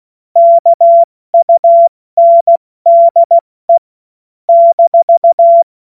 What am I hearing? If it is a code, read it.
Morse code, heard as a keyed tone, about 16 wpm, spaced as for 14 wpm, 680 Hz. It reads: KUNDE -
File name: Morsekurs / Witze / Witz 13. Juli 2021, 70 F80